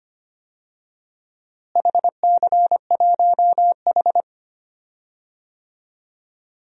Morse code, heard as a keyed tone, about 25 wpm, 700 Hz. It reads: HC1H
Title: DXCC-100 Call Signs - 4 Characters - Repeated 2x - 25wpm